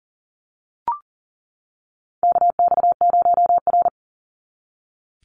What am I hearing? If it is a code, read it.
Morse code, heard as a keyed tone, about 37 wpm, 700 Hz. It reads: KX0R
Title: NA SOTA Activator Callsigns 40wpm